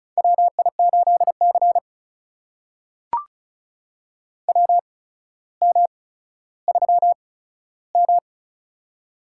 Morse code, heard as a keyed tone, about 36 wpm, spaced as for 15 wpm, 700 Hz. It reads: WI8C WM3M
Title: USA Call Signs - 4 Characters - 8x Character Spacing - 35wpm